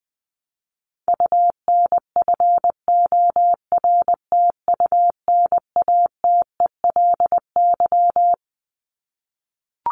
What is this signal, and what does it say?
Morse code, heard as a keyed tone, about 20 wpm, 700 Hz. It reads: UNFORTUNATELY
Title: Top 1751-2000 Words 20wpm